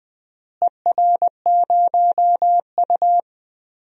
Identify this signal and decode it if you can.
Morse code, heard as a keyed tone, about 20 wpm, 700 Hz. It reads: ER0U
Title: DXCC-100 Call Signs - 4 Characters - Repeated 2x - 20wpm